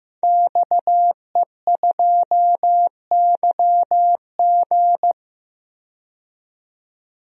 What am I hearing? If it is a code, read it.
Morse code, heard as a keyed tone, about 15 wpm, 700 Hz. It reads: XE2YG
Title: DXCC-100 Call Signs - 5 Characters - 15wpm